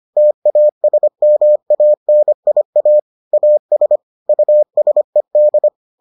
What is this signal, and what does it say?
Morse code, heard as a keyed tone, about 25 wpm, 600 Hz. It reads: TASMANIA AS USED